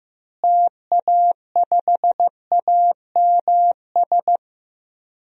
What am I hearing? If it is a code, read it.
Morse code, heard as a keyed tone, about 15 wpm, 700 Hz. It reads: TA5AMS
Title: DXCC-100 Call Signs - 6 Characters - Repeated 3x - 15wpm